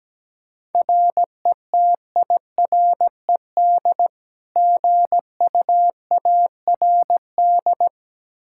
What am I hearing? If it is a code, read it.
Morse code, heard as a keyed tone, about 17 wpm, 700 Hz. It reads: RETIRED GUARD
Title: Practice QSO - Exchange 3 - More Info - 17wpm